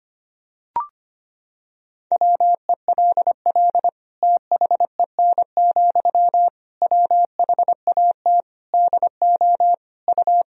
Morse code, heard as a keyed tone, about 25 wpm, 700 Hz. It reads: WELL THEN, WHAT DO U
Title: Sentences from Top 200 Words and Abbreviations 25wpm